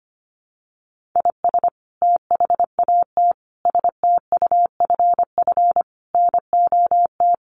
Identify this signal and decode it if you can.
Morse code, heard as a keyed tone, about 25 wpm, 700 Hz. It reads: IS THAT STUFF NOT
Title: Sentences from Top 1400 Words 25wpm